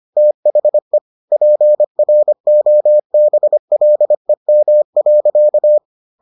Morse code, heard as a keyed tone, about 25 wpm, 600 Hz. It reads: THE PROBLEM.